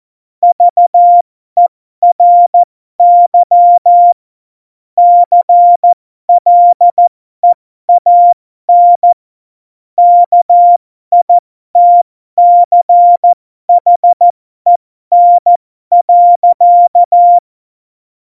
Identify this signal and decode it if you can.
Morse code, heard as a keyed tone, about 14 wpm, spaced as for 12 wpm, 700 Hz. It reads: VERY CLEAN KITCHEN.
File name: Morsekurs / Witze